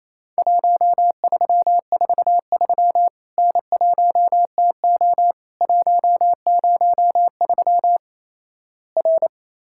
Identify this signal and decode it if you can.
Morse code, heard as a keyed tone, about 28 wpm, 700 Hz. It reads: 1343 N1TO 103 R
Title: Worked All Europe DX Contest - QTC Exchange - 28wpm